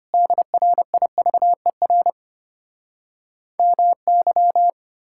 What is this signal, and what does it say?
Morse code, heard as a keyed tone, about 33 wpm, 700 Hz. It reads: DRIVER MY